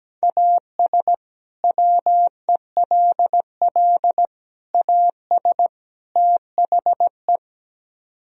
Morse code, heard as a keyed tone, about 17 wpm, 700 Hz. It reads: AS WELL AS THE
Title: Speed-Racing - Sets of 4 Words 17wpm